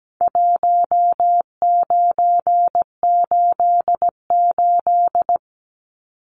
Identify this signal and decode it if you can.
Morse code, heard as a keyed tone, about 17 wpm, 700 Hz. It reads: 1988
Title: Numbers - Contest Style - 4-Digit 17wpm